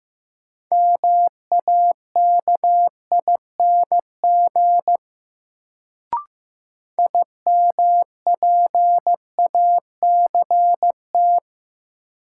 Morse code, heard as a keyed tone, about 15 wpm, 700 Hz. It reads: MAKING IMPACT
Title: Sentences from Top 1000 Words - Review Part 6 - New Word Drill - 15wpm